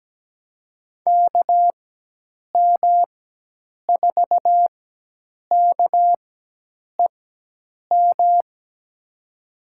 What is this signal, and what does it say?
Morse code, heard as a keyed tone, about 17 wpm, spaced as for 8 wpm, 700 Hz. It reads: KM4KEM
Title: USA Call Signs - 6 Characters - 4x Character Spacing - 17wpm